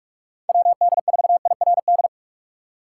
Morse code, heard as a keyed tone, about 60 wpm, spaced as for 52 wpm, 700 Hz. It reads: WD4IRD
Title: Speed Racing - USA 2x3 Call Signs 45wpm